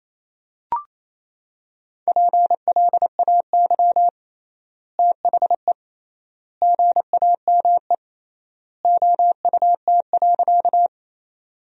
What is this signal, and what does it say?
Morse code, heard as a keyed tone, about 27 wpm, 700 Hz. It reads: PLAY THE GAME OUT.
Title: 3x Word-Spacing - Sentences from Top 500 Words - 28wpm